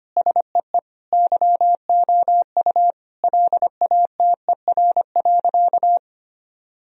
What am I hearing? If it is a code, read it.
Morse code, heard as a keyed tone, about 25 wpm, 700 Hz. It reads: SEE YOU LATER.
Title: Sentences from Top 1000 Words - Review Part 4 - 25wpm